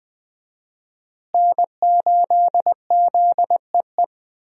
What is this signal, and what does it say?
Morse code, heard as a keyed tone, about 20 wpm, 700 Hz. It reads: N8ZEE
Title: USA Call Signs - 5 Characters - Repeated 3x - 20wpm